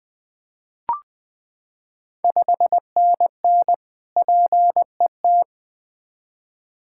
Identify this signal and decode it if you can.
Morse code, heard as a keyed tone, about 20 wpm, 690 Hz. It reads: 5NN PET
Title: Ontario QSO Party Exchange 20wpm